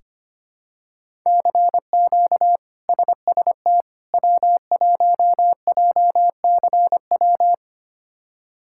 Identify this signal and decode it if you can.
Morse code, heard as a keyed tone, about 25 wpm, 700 Hz. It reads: CQ SST W1JCW
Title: K1USN SST QSO Practice - 25wpm